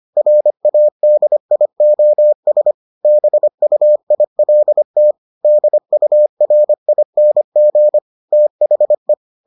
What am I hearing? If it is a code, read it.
Morse code, heard as a keyed tone, about 25 wpm, 600 Hz. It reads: RADIOS BUILT DURING THE